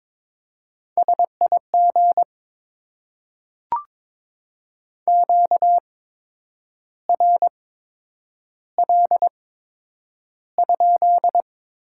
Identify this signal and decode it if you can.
Morse code, heard as a keyed tone, about 22 wpm, spaced as for 8 wpm, 700 Hz. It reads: SIG QRL?
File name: Top 90 QSO Elements - 8x Character Spacing - 22wpm